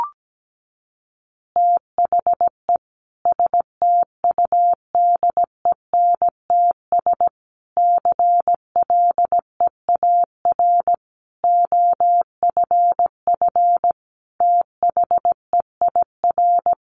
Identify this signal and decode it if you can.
Morse code, heard as a keyed tone, about 17 wpm, 700 Hz. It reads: THE STUDENTS CLEAR OFF THEIR
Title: Additional Easy Phrases 17wpm